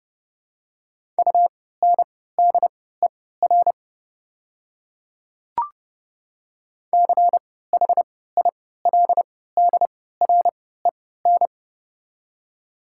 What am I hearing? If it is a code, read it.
Morse code, heard as a keyed tone, about 32 wpm, spaced as for 17 wpm, 700 Hz. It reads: UNDER CHILDREN
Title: 3x Character Spacing - Top 300 Words - 30wpm